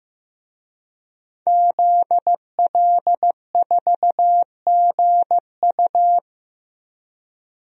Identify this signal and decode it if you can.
Morse code, heard as a keyed tone, about 15 wpm, 700 Hz. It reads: ZL4GU
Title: DXCC-100 Call Signs - 5 Characters - 15wpm